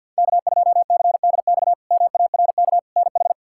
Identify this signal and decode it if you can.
Morse code, heard as a keyed tone, about 52 wpm, 700 Hz. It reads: K2CDX MARK NH
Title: K1USN SST QSO Practice - 50wpm